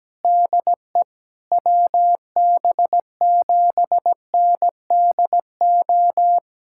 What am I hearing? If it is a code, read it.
Morse code, heard as a keyed tone, about 17 wpm, 700 Hz. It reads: DE WB7NDO
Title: USA Callsign DE Callsign 17wpm